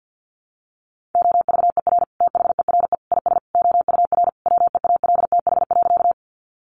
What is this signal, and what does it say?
Morse code, heard as a keyed tone, about 50 wpm, 700 Hz. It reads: OVER THERE IS OUR WEALTH.
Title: Sentences from Top 1200 Words 50wpm